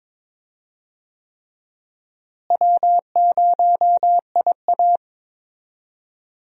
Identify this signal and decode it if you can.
Morse code, heard as a keyed tone, about 22 wpm, 700 Hz. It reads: W0IA
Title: USA Call Signs - 4 Characters - Repeated 3x - 22wpm